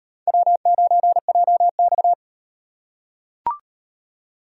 Morse code, heard as a keyed tone, about 34 wpm, 700 Hz. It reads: W9JX